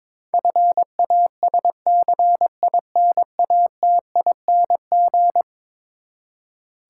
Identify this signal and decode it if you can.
Morse code, heard as a keyed tone, about 22 wpm, 700 Hz. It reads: FASCINATING
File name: practice / 22wpm / Eleven Letter Words - 22wpm